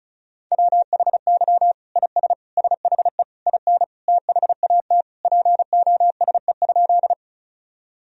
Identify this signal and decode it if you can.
Morse code, heard as a keyed tone, about 34 wpm, 700 Hz. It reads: WHY IS SHE IN THAT POSE?
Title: Sentences from Top 1000 Words - Review Part 2 - 35wpm